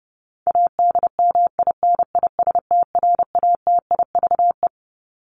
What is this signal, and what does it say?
Morse code, heard as a keyed tone, about 30 wpm, 700 Hz. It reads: ADMINISTRATIVE
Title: Top 1501-1750 Words 30wpm